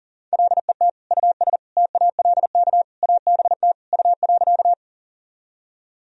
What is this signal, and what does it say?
Morse code, heard as a keyed tone, about 40 wpm, 700 Hz. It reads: LET US TALK ABT U.
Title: Sentences from Top 400 Words and Abbreviations 40wpm